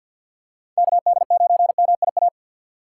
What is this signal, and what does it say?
Morse code, heard as a keyed tone, about 51 wpm, 690 Hz. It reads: KD9MIA
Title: USA Call Signs - 4x Character Spacing - 50wpm